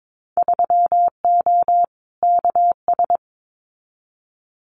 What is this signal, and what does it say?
Morse code, heard as a keyed tone, about 22 wpm, 700 Hz. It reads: 3O KS
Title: Winter Field Day Exchanges Encore 22wpm